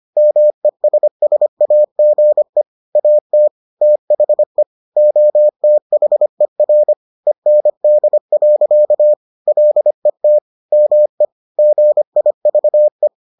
Morse code, heard as a keyed tone, about 25 wpm, 600 Hz. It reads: MESSAGE AT THE OTHER END. LET ME GIVE